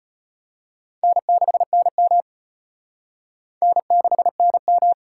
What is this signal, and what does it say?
Morse code, heard as a keyed tone, about 33 wpm, 700 Hz. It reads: N6NM N6NM